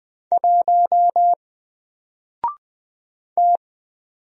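Morse code, heard as a keyed tone, about 20 wpm, 700 Hz. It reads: 1 T
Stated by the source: Single letter and number Morse drill at 20 wpm, from N4LCD.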